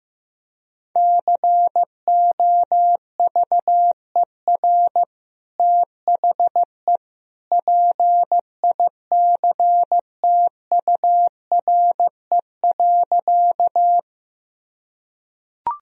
Morse code, heard as a keyed tone, about 16 wpm, 700 Hz. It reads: COVER THE PICTURE.
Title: Sentences from Top 1000 Words - Review Part 1 - 15wpm